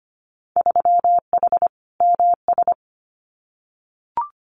Morse code, heard as a keyed tone, about 25 wpm, 700 Hz. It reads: 3H MS
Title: Winter Field Day Exchanges Encore 25wpm